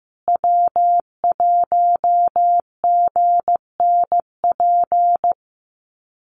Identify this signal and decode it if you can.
Morse code, heard as a keyed tone, about 15 wpm, 700 Hz. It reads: W1GNP
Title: USA Call Signs Repeated 3x 15wpm